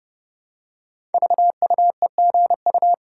Morse code, heard as a keyed tone, about 30 wpm, 700 Hz. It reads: VUEGU
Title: Code Groups - 5 Letters - 30wpm